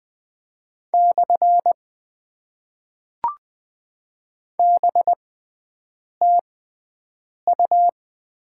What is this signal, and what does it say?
Morse code, heard as a keyed tone, about 20 wpm, spaced as for 7 wpm, 700 Hz. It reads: / BTU
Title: Top 90 QSO Elements - 6x Character Spacing - 20wpm